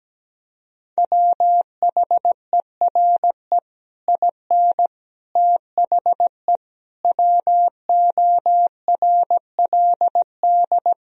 Morse code, heard as a keyed tone, about 17 wpm, 700 Hz. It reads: WHERE IN THE WORLD
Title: Sentences from Top 200 Words and Abbreviations 17wpm